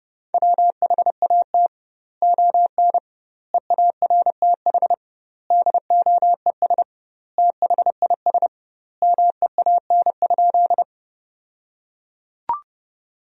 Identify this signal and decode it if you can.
Morse code, heard as a keyed tone, about 30 wpm, 700 Hz. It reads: WHAT ON EARTH DOES THIS MEAN?
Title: 2x Word-Spacing - Sentences from Top 200 Words - 30wpm